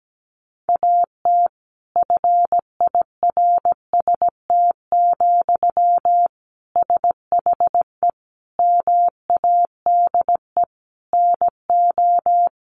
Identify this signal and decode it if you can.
Morse code, heard as a keyed tone, about 17 wpm, 700 Hz. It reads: AT FIRST, SHE MADE NO
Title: Sentences from Top 1400 Words 17wpm